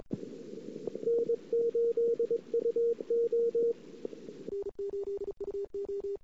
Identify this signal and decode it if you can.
Morse code, heard as a keyed tone, about 22 wpm, 470 Hz. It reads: N8UO
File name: CW / 2008-ss-cw-k5tr-op-n5rz -21:00